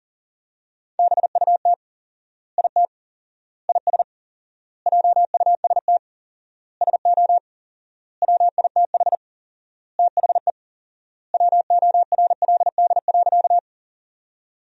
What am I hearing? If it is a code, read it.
Morse code, heard as a keyed tone, about 40 wpm, 700 Hz. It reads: BUT IT IS JUST SO WITH THE WORLD.
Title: 4x Word-Spacing - Sentences from Top 200 Words - 40wpm